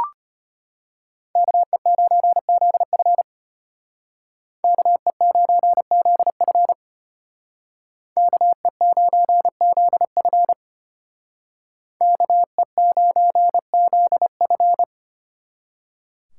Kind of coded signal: Morse code

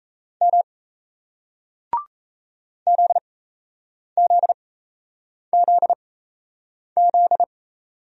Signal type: Morse code

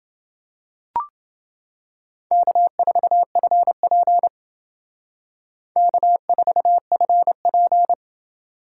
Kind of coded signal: Morse code

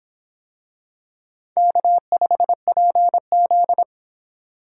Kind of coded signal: Morse code

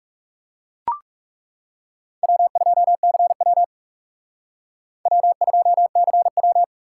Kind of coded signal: Morse code

